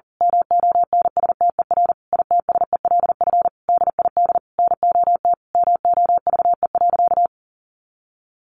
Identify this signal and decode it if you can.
Morse code, heard as a keyed tone, about 40 wpm, 700 Hz. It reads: MONSTER ITSELF DID NOT MOVE.
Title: A Sound Of Thunder 40wpm